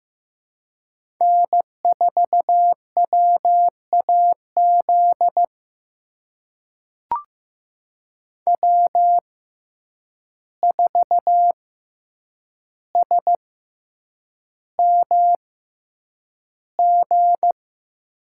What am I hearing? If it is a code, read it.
Morse code, heard as a keyed tone, about 15 wpm, spaced as for 8 wpm, 700 Hz. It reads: N4WAZ W4SMG
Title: USA Call Signs - 5 Characters - 6x Character Spacing - 15wpm